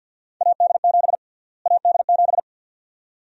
Morse code, heard as a keyed tone, about 51 wpm, 700 Hz. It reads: AD7 AD7
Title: USA Call Sign Prefixes - Repeated 2x - 50wpm